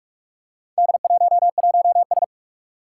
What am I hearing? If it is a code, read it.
Morse code, heard as a keyed tone, about 41 wpm, 700 Hz. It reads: D11S
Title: Four Challenging Letter-Number Combinations 45wpm